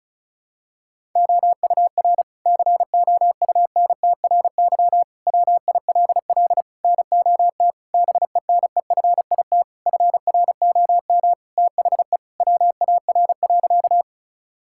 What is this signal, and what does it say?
Morse code, heard as a keyed tone, about 36 wpm, 700 Hz. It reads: OUR COUNTRY WILL NOT BENEFIT FROM THE WAR.
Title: Sentences from Top 1000 Words - Review Part 6 - 35wpm